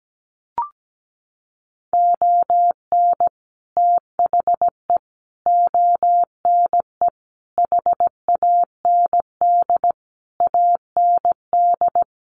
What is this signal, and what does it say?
Morse code, heard as a keyed tone, about 17 wpm, 700 Hz. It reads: ON THE ONE HAND AND
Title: Sets of 5 Words Encore2 17wpm